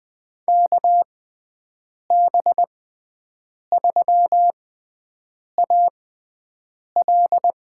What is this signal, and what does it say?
Morse code, heard as a keyed tone, about 20 wpm, spaced as for 7 wpm, 700 Hz. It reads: KB3AL
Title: USA Call Signs - 6x Character Spacing - 20wpm